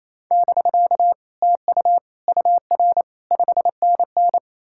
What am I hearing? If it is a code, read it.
Morse code, heard as a keyed tone, about 38 wpm, spaced as for 34 wpm, 700 Hz. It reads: <BK> TU UR 5NN
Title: Speed-Racing Common QSO Phrases Encore - 28wpm